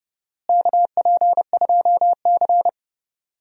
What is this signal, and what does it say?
Morse code, heard as a keyed tone, about 32 wpm, 700 Hz. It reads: KP2C
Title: DXCC-100 Call Signs - 4 Characters - Repeated 3x - 30wpm